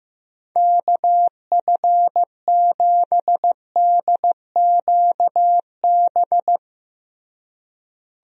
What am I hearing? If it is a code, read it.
Morse code, heard as a keyed tone, about 15 wpm, 700 Hz. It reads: KF7DQB
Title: USA Call Signs - 6 Characters - Repeated 2x - 15wpm